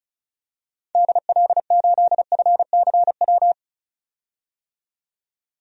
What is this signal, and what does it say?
Morse code, heard as a keyed tone, about 36 wpm, 700 Hz. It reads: DL8FCW
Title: DXCC-100 Call Signs - 6 Characters - 35wpm